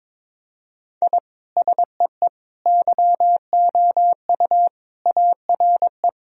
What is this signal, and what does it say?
Morse code, heard as a keyed tone, about 22 wpm, 700 Hz. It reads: I SEE YOU ARE
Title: Sentences from Top 1000 Words - Review Part 3 - 22wpm